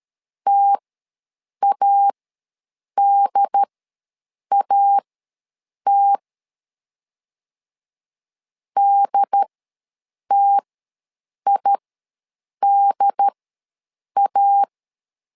Morse code, heard as a keyed tone, about 13 wpm, spaced as for 7 wpm, 800 Hz. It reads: TADAT DTIDA
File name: LEZIONI / 2 UHDN / 2 UHD ws 12wpm 1722